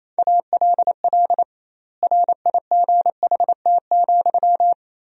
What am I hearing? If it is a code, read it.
Morse code, heard as a keyed tone, about 28 wpm, 700 Hz. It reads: ALL RIGHT,
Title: 2x Word-Spacing - Sentences from Top 200 Words - 28wpm